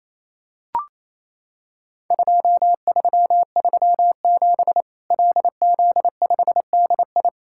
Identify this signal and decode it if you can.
Morse code, heard as a keyed tone, about 28 wpm, 700 Hz. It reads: 2337 LZ5DI
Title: Worked All Europe DX Contest - QTC Exchange - 28wpm